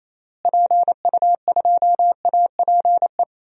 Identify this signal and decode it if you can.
Morse code, heard as a keyed tone, about 28 wpm, 700 Hz. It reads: PU2APE
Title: DXCC-100 Call Signs - 6 Characters - Repeated 3x - 28wpm